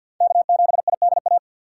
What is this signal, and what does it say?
Morse code, heard as a keyed tone, about 52 wpm, 700 Hz. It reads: K7IDA
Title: USA Call Signs - 3x Character Spacing - 50wpm